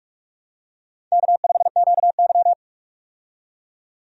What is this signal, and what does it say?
Morse code, heard as a keyed tone, about 46 wpm, 700 Hz. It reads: K5QY
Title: USA Call Signs - 4 Characters - 45wpm